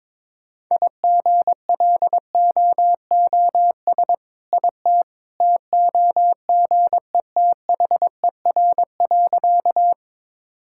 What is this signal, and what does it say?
Morse code, heard as a keyed tone, about 22 wpm, 690 Hz. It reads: IGLOOS IT TOGETHER.
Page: morse-code-ninja-practice-22wpm